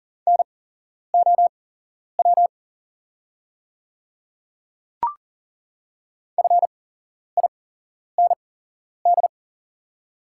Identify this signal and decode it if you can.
Morse code, heard as a keyed tone, about 38 wpm, spaced as for 11 wpm, 700 Hz. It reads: NOW FIND
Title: 8x Character Spacing - Top 200 Words - 40wpm